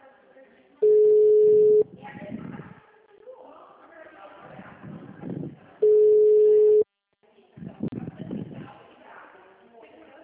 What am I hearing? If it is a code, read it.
Ringback tone (Europe, 425 Hz), with speech.
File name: out- -2001- - - .30682